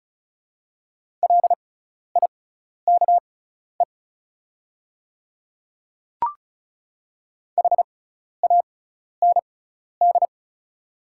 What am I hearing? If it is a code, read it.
Morse code, heard as a keyed tone, about 36 wpm, spaced as for 12 wpm, 690 Hz. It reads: LIKE HAND
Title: 6x Character Spacing - Top 500 Words - 35wpm